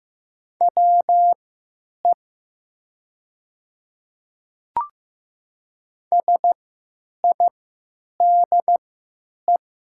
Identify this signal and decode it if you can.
Morse code, heard as a keyed tone, about 15 wpm, spaced as for 9 wpm, 700 Hz. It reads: WE SIDE
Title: 3x Character Spacing - Top 100 Words - 15wpm